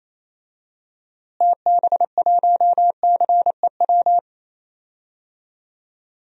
Morse code, heard as a keyed tone, about 28 wpm, 700 Hz. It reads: TB1CEW
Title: DXCC-100 Call Signs - 6 Characters - Repeated 2x - 28wpm